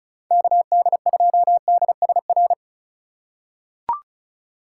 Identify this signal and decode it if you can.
Morse code, heard as a keyed tone, about 36 wpm, 700 Hz. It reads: KD2DSR
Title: USA Call Signs - 6 Characters - 8x Character Spacing - 35wpm